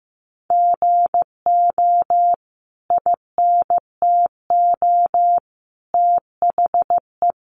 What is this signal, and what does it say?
Morse code, heard as a keyed tone, about 15 wpm, 700 Hz. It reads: GO INTO THE
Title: Sentences from Top 1300 Words 15wpm